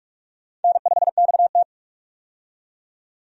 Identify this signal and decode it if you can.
Morse code, heard as a keyed tone, about 41 wpm, 700 Hz. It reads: N5XT
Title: USA Call Signs - 4 Characters - 45wpm